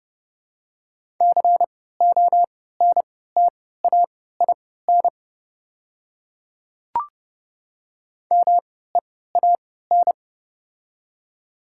Morse code, heard as a keyed tone, about 32 wpm, spaced as for 17 wpm, 700 Hz. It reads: CONTAIN MEAN
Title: 3x Character Spacing - Top 500 Words - 30wpm